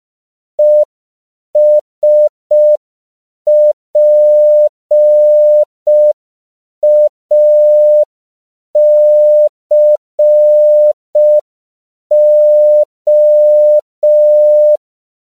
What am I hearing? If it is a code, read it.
Morse code, heard as a keyed tone, about 5 wpm, 600 Hz. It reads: ESPACO
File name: CW / 5PPM